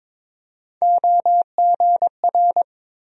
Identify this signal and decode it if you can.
Morse code, heard as a keyed tone, about 22 wpm, 700 Hz. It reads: OGR
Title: USA Call Sign Postfixes - 3 Letters 22wpm